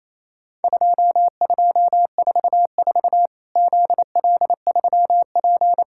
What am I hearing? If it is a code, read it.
Morse code, heard as a keyed tone, about 28 wpm, 700 Hz. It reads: 2244 ZL3P
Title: Worked All Europe DX Contest - QTC Exchange - 28wpm